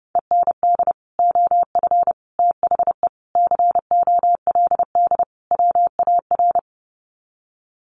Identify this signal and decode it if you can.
Morse code, heard as a keyed tone, about 30 wpm, 700 Hz. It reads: END OF THE COLD WAR
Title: Sets of 5 Words Encore 30wpm